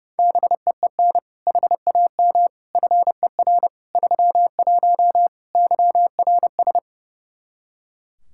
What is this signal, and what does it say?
Morse code, heard as a keyed tone, about 30 wpm, 700 Hz. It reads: BEEN HAM FER 31 YRS
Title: K1USN SST QSO Practice 30wpm